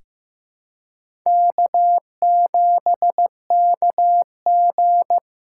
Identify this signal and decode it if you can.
Morse code, heard as a keyed tone, about 15 wpm, 700 Hz. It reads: K7KG
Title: USA Call Signs - 4x Character Spacing - 15wpm